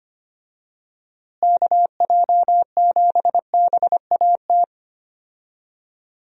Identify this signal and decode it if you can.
Morse code, heard as a keyed tone, about 25 wpm, 700 Hz. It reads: KJ7BAT
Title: USA Call Signs - 6 Characters - Repeated 3x - 25wpm